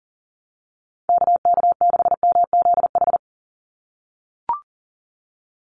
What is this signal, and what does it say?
Morse code, heard as a keyed tone, about 40 wpm, 700 Hz. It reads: KK6MZH
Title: USA Call Signs 40wpm